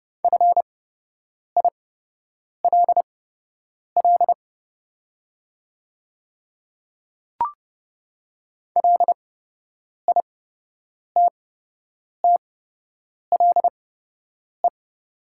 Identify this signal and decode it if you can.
Morse code, heard as a keyed tone, about 30 wpm, spaced as for 8 wpm, 700 Hz. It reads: FILL LITTLE
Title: 8x Character Spacing - Top 500 Words - 30wpm